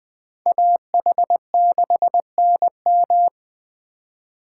Morse code, heard as a keyed tone, about 20 wpm, 700 Hz. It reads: AH6NM